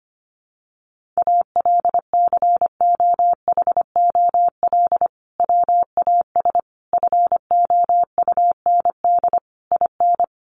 Morse code, heard as a keyed tone, about 25 wpm, 700 Hz. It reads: ALCOHOL WAS FOUND IN
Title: Sentences from Top 1400 Words 25wpm